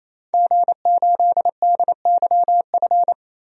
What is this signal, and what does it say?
Morse code, heard as a keyed tone, about 28 wpm, 700 Hz. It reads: G8DYF
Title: DXCC-100 Call Signs - 5 Characters - 28wpm